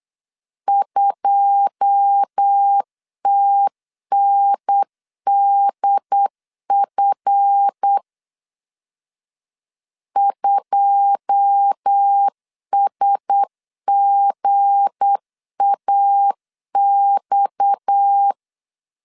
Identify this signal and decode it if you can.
Morse code, heard as a keyed tone, about 8 wpm, 800 Hz. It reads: 2TNDF 2SGAX
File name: LEZIONI / 9 2 8wpm 3437